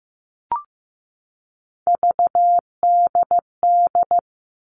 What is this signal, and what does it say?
Morse code, heard as a keyed tone, about 15 wpm, 700 Hz. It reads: VDD
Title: All Cut Numbers - 3-Digit 15wpm